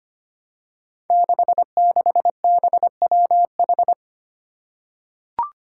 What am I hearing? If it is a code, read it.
Morse code, heard as a keyed tone, about 25 wpm, 700 Hz. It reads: 66BWH
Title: Five Challenging Letter-Number Combinations 25wpm